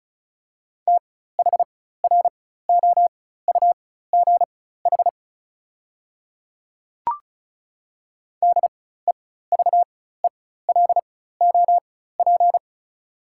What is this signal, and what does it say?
Morse code, heard as a keyed tone, about 36 wpm, spaced as for 16 wpm, 690 Hz. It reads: THROUGH DEVELOP
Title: 4x Character Spacing - Top 500 Words - 35wpm